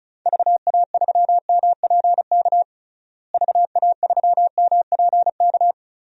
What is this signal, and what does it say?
Morse code, heard as a keyed tone, about 36 wpm, 700 Hz. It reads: VA3MPK VA3MPK
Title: DXCC-100 Call Signs - 6 Characters - Repeated 2x - 35wpm